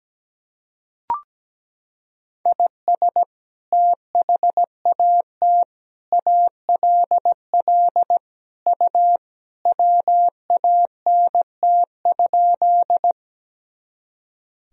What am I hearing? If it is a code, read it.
Morse code, heard as a keyed tone, about 17 wpm, 700 Hz. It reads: IS THAT ALL U WANT?
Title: Sentences from Top 200 Words and Abbreviations 17wpm